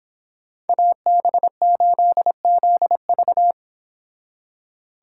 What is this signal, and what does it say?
Morse code, heard as a keyed tone, about 25 wpm, 700 Hz. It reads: AB8ZV